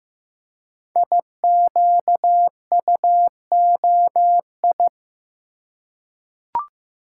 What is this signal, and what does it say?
Morse code, heard as a keyed tone, about 15 wpm, 700 Hz. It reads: IQUOI
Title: Code Groups - 5 Letters - 15wpm